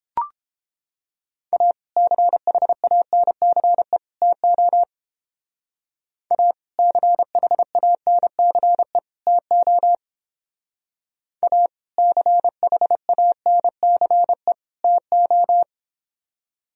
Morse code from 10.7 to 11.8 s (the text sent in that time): A